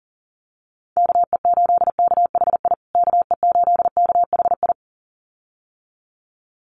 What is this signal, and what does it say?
Morse code, heard as a keyed tone, about 37 wpm, 700 Hz. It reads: KE8KHI KE8KHI
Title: USA Call Signs Repeated 2x 40wpm